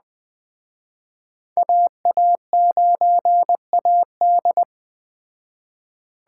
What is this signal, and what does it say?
Morse code, heard as a keyed tone, about 20 wpm, 700 Hz. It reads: AA9AD
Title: Speed Racing - USA 2x2 Call Signs 17wpm